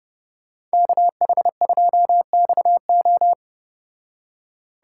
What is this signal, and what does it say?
Morse code, heard as a keyed tone, about 33 wpm, 700 Hz. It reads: KH2XO